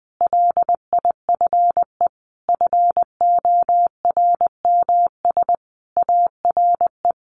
Morse code, heard as a keyed tone, about 20 wpm, 700 Hz. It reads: LIFE FORMS ARE
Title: A Sound Of Thunder 20wpm